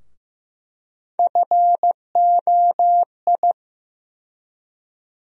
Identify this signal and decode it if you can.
Morse code, heard as a keyed tone, about 15 wpm, 700 Hz. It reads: FOI
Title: USA Call Sign Postfixes - 3 Letters 15wpm